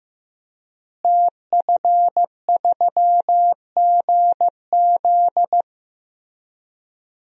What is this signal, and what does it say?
Morse code, heard as a keyed tone, about 15 wpm, 700 Hz. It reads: TF3GZ
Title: DXCC-100 Call Signs - 5 Characters - Repeated 3x - 15wpm